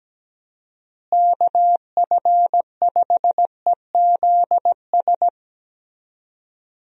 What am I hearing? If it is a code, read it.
Morse code, heard as a keyed tone, about 17 wpm, 700 Hz. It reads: KF5EZS
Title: USA Call Signs - 6 Characters - Repeated 3x - 17wpm